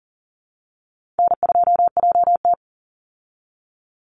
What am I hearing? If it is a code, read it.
Morse code, heard as a keyed tone, about 40 wpm, 700 Hz. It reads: N2JT
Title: Easy Call Signs Repeated 2x - 40wpm